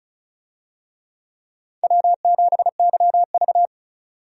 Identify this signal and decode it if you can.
Morse code, heard as a keyed tone, about 36 wpm, 700 Hz. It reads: W7YV
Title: USA Call Signs - 4 Characters - 35wpm